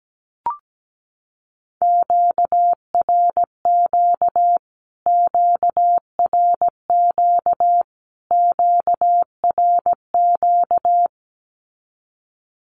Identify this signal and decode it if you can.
Morse code, heard as a keyed tone, about 17 wpm, 700 Hz. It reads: QRQ QRQ QRQ
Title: Q-Codes Repeated 3x 17wpm